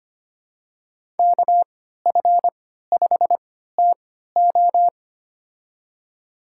Morse code, heard as a keyed tone, about 25 wpm, spaced as for 14 wpm, 700 Hz. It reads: KF5TO
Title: USA Call Signs - 5 Characters - 3x Character Spacing - 25wpm